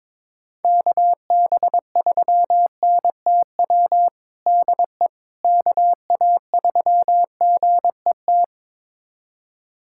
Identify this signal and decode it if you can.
Morse code, heard as a keyed tone, about 22 wpm, 700 Hz. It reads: KB3NTW DE KA3GET
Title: USA Callsign DE Callsign - 6 Characters - 22wpm